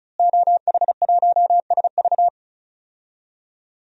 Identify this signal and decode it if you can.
Morse code, heard as a keyed tone, about 36 wpm, 700 Hz. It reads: OH1SV
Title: DXCC-100 Call Signs - 5 Characters - Repeated 2x - 35wpm